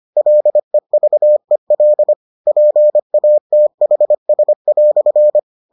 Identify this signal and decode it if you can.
Morse code, heard as a keyed tone, about 25 wpm, 600 Hz. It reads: LEVEL PATHS"